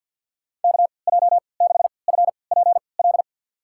Morse code, heard as a keyed tone, about 49 wpm, spaced as for 28 wpm, 700 Hz. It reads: KJ6FPL
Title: USA Call Signs - 3x Character Spacing - 50wpm